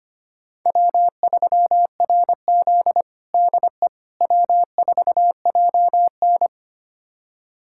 Morse code, heard as a keyed tone, about 25 wpm, 700 Hz. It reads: W3RZ DE W4JN
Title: USA Callsign DE Callsign - 4 Characters - 25wpm